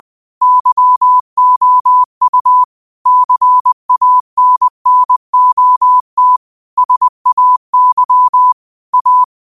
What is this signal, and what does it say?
Morse code, heard as a keyed tone, about 20 wpm, 1000 Hz. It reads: YOU CANNOT SAY A